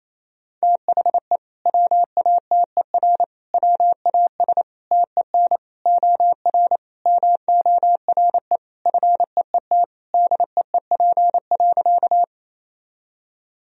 Morse code, heard as a keyed tone, about 28 wpm, 700 Hz. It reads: THE WATER WAS TEN OR MORE FEET DEEP.
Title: Sentences from Top 1000 Words - Review Part 3 - 28wpm